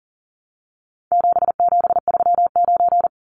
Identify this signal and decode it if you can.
Morse code, heard as a keyed tone, about 40 wpm, 700 Hz. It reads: 7739
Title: Numbers - 4-Digit 40wpm